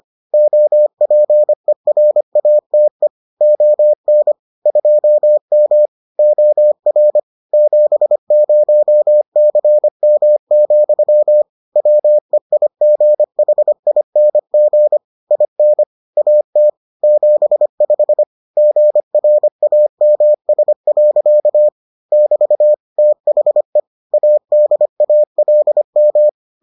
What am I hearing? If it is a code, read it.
Morse code, heard as a keyed tone, about 25 wpm, 600 Hz. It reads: OPERATE ON 2M OR 70CM, WEIGHING IN AT 75 GRAMS. <BT> THE ADALM